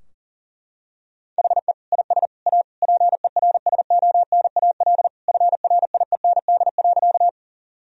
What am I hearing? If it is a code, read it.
Morse code, heard as a keyed tone, about 40 wpm, 690 Hz. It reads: HE IS A PERSONAL FRIEND.
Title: Sentences from Top 1000 Words - Review Part 4 - 40wpm